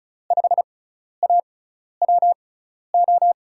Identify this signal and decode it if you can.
Morse code, heard as a keyed tone, about 36 wpm, spaced as for 12 wpm, 700 Hz. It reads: 5AWO
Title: USA Call Signs - 6 Characters - 6x Character Spacing - 35wpm